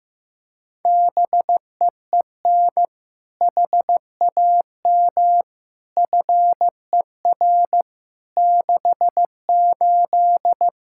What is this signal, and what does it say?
Morse code, heard as a keyed tone, about 15 wpm, 700 Hz. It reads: BEEN HAM FER 68
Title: Practice QSO - Exchange 3 - More Info - 15wpm